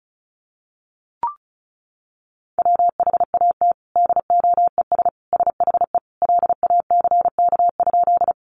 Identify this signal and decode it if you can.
Morse code, heard as a keyed tone, about 34 wpm, 700 Hz. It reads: WHAT DOES SHE LACK?
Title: Sentences from Top 1100 Words 35wpm